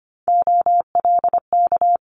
Morse code, heard as a keyed tone, about 25 wpm, 700 Hz. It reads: OLK
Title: Multi-letter Phonemes Quick Fire 25wpm